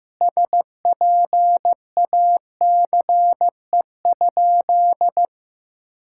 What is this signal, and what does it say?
Morse code, heard as a keyed tone, about 15 wpm, 700 Hz. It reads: SPACE?
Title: Sentences from Top 1000 Words - Review Part 5 - 15wpm